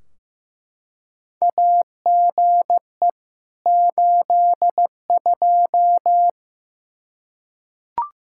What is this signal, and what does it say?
Morse code, heard as a keyed tone, about 16 wpm, 700 Hz. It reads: AGE 82
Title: K1USN SST QSO Practice 15wpm